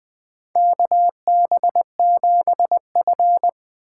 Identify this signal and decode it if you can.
Morse code, heard as a keyed tone, about 20 wpm, 700 Hz. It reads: KB7F
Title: USA Call Signs - 4 Characters - 3x Character Spacing - 20wpm